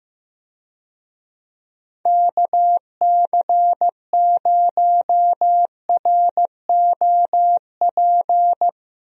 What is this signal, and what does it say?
Morse code, heard as a keyed tone, about 15 wpm, 700 Hz. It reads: KC0ROP
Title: USA Call Signs - 6 Characters - Repeated 2x - 15wpm